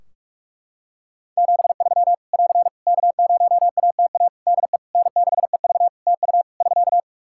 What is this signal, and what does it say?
Morse code, heard as a keyed tone, about 44 wpm, 700 Hz. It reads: 73 <AR> K0ATA DE N6EV TU <SK>
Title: Practice QSO - Exchange 4 - Ending - 45wpm